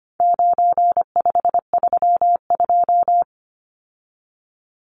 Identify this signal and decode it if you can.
Morse code, heard as a keyed tone, about 25 wpm, 700 Hz. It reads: 9532
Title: Numbers - 4-Digit 25wpm